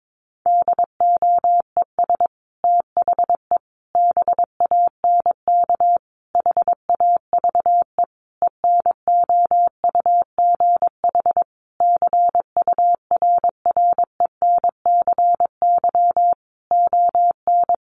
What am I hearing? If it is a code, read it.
Morse code, heard as a keyed tone, about 22 wpm, 700 Hz. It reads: DOES THE BANK HAVE ENOUGH CURRENCY ON
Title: Sentences from Top 1500 Words 22wpm